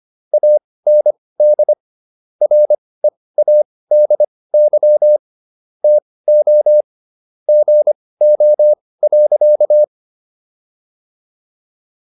Morse code, heard as a keyed tone, about 25 wpm, spaced as for 18 wpm, 600 Hz. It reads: AND READY TO GO.